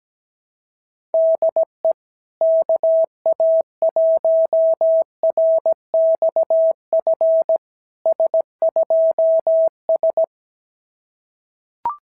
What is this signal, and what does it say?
Morse code, heard as a keyed tone, about 17 wpm, 690 Hz. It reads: DE KA1RXF S2S
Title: SOTA QSO Practice 17wpm